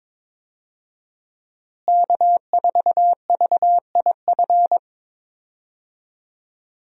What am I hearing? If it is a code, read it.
Morse code, heard as a keyed tone, about 22 wpm, 700 Hz. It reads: K4VIF
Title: USA Call Signs - 5 Characters - Repeated 2x - 22wpm